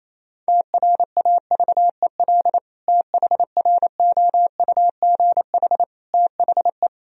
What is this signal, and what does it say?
Morse code, heard as a keyed tone, about 28 wpm, 700 Hz. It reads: TRAVEL THROUGH THE
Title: Sentences from Top 1000 Words - Review Part 2 - 28wpm